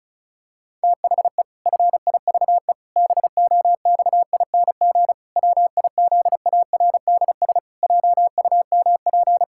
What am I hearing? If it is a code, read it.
Morse code, heard as a keyed tone, about 34 wpm, 700 Hz. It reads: THE FIVE BOXING WIZARDS JUMP
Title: Pangram Challenge v2 - 35wpm